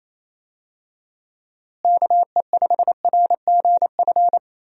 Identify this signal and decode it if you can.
Morse code, heard as a keyed tone, about 27 wpm, 700 Hz. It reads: KE5RGF
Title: USA Call Signs - 6 Characters - Repeated 2x - 28wpm